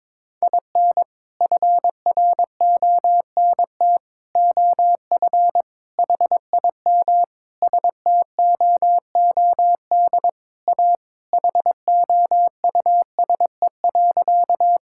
Morse code, heard as a keyed tone, about 21 wpm, 700 Hz. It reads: IN FRONT OF HIM STOOD A HOUSE.
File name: practice / 22wpm / Sentences with 4-Word Phrases - 22wpm